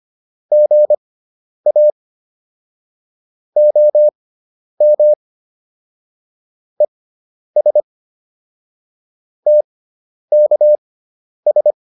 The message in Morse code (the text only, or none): GA OM ES TKS